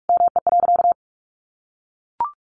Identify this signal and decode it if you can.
Morse code, heard as a keyed tone, about 44 wpm, 700 Hz. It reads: ME.
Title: Sentences from Top 1200 Words 45wpm